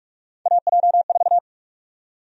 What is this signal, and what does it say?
Morse code, heard as a keyed tone, about 43 wpm, 700 Hz. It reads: AJ4
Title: USA Call Sign Prefixes - Repeated 2x - 45wpm